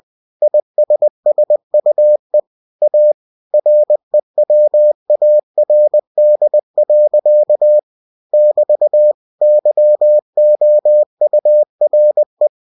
Morse code, heard as a keyed tone, about 20 wpm, 600 Hz. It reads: ISSUE A REWARD. <BT> YOURE
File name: cw-books / winnie-the-pooh / winnie-20wpm-page-0030